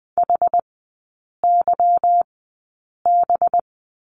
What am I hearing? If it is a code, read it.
Morse code, heard as a keyed tone, about 20 wpm, spaced as for 8 wpm, 700 Hz. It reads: HYB
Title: 184 Lesson 30e - Letter b - Three Characters 20wpm